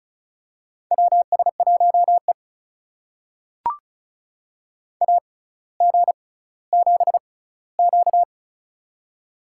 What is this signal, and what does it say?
Morse code, heard as a keyed tone, about 36 wpm, spaced as for 18 wpm, 700 Hz. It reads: WS1E AG7Q
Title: USA Call Signs - 4 Characters - 6x Character Spacing - 35wpm